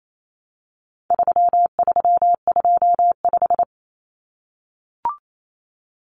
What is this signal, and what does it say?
Morse code, heard as a keyed tone, about 28 wpm, 700 Hz. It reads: 3325
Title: Common Cut Numbers - 4-Digit 28wpm